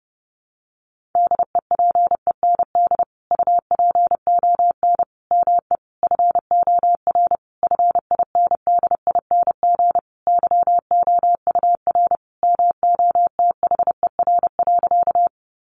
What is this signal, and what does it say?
Morse code, heard as a keyed tone, about 30 wpm, 700 Hz. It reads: DEPEND UPON ME FOR FINDING YOUR MOTHER.
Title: Sentences from Top 1300 Words 30wpm